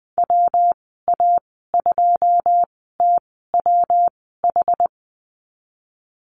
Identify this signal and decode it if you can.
Morse code, heard as a keyed tone, about 19 wpm, spaced as for 14 wpm, 700 Hz. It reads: WA2TWH
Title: 154 Lesson 25g - Letter p - Call Signs 20wpm